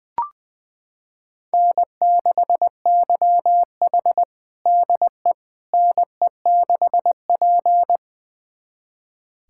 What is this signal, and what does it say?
Morse code, heard as a keyed tone, about 20 wpm, 700 Hz. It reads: N6YH DE NE6P
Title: USA Callsign DE Callsign - 4 Characters - 20wpm